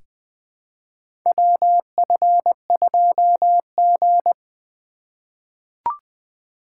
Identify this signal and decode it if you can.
Morse code, heard as a keyed tone, about 21 wpm, 700 Hz. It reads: WF2G
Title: USA Call Signs - 2x Character Spacing - 20wpm